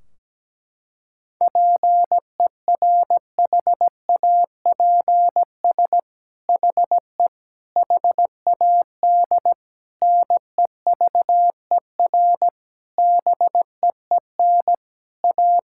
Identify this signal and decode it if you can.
Morse code, heard as a keyed tone, about 17 wpm, 700 Hz. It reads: PERHAPS HE HAD NEVER BEEN A
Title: Sentences from Top 1600 Words 17wpm